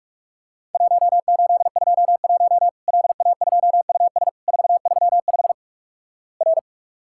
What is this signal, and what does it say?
Morse code, heard as a keyed tone, about 46 wpm, 700 Hz. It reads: 1821 LA2US 435 R
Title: Worked All Europe DX Contest - QTC Exchange - 45wpm